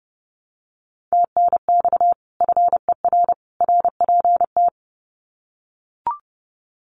Morse code, heard as a keyed tone, about 30 wpm, 700 Hz. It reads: TNX FER RPT
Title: Common QSO Phrases 30wpm